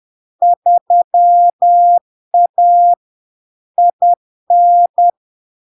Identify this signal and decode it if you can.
Morse code, heard as a keyed tone, about 10 wpm, 700 Hz. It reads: 3A IN
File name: LARK-FD-2022-Log-in-CW-10wpm-700-Hz